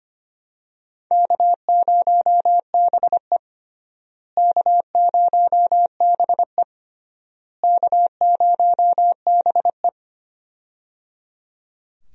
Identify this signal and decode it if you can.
Morse code, heard as a keyed tone, about 25 wpm, 700 Hz. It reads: K0BE K0BE K0BE
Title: USA Call Signs - 4 Characters - Repeated 3x - 25wpm